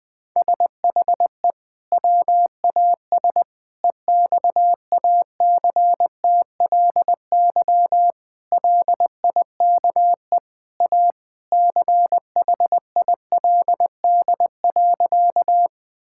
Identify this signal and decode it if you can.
Morse code, heard as a keyed tone, about 20 wpm, 700 Hz. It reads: SHE WAS EXACTLY LIKE A CHILD.
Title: Sentences from Top 1000 Words - Review Part 6 - 20wpm